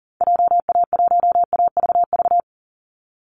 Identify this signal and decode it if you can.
Morse code, heard as a keyed tone, about 40 wpm, 700 Hz. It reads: JA1AVV
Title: USA Call Signs 40wpm